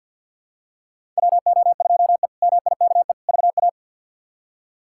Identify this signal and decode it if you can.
Morse code, heard as a keyed tone, about 52 wpm, 700 Hz. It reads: WO2E MIKE VA
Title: K1USN SST QSO Practice - 50wpm